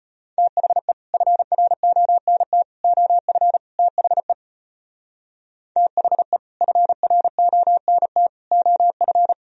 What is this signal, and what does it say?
Morse code, heard as a keyed tone, about 36 wpm, 700 Hz. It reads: THE FRONT OF THE THE FRONT OF